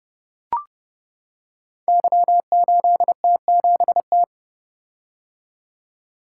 Morse code, heard as a keyed tone, about 29 wpm, 700 Hz. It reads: Y8T7T
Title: Code Groups - 5 Letter:Numbers - 30wpm